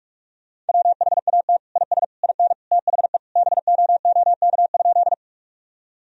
Morse code, heard as a keyed tone, about 42 wpm, 700 Hz. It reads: WHAT IS IN THE BOOK?
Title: Sentences from Top 1000 Words - Review Part 1 - 45wpm